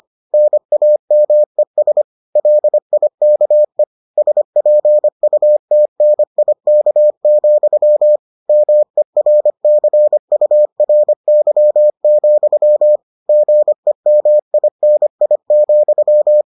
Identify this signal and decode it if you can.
Morse code, heard as a keyed tone, about 25 wpm, 600 Hz. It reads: NAMES LIKE SPUTNIK, MERCURY, GEMINI,